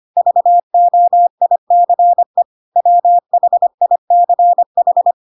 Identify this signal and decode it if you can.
Morse code, heard as a keyed tone, about 25 wpm, 700 Hz. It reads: VOICE WHICH
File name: Pinocchio / Pinocchio ch2 25 25 ik1hge